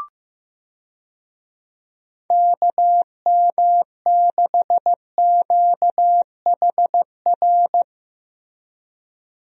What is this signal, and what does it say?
Morse code, heard as a keyed tone, about 15 wpm, 700 Hz. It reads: KM6QHR
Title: USA Call Signs - 6 Characters - Repeated 2x - 15wpm